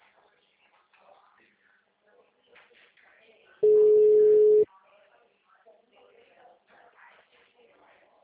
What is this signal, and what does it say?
Ringback tone (Europe, 425 Hz), with speech.